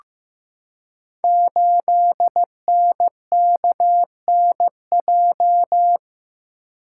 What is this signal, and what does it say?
Morse code, heard as a keyed tone, about 15 wpm, 700 Hz. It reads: 8NKNJ
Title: Code Groups - 5 Letter:Numbers - 15wpm